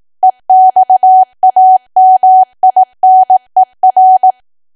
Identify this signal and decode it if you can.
Morse code, heard as a keyed tone, about 18 wpm, 750 Hz. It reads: EXAMINER
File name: cwa / W1AW-Level2 / 18WPM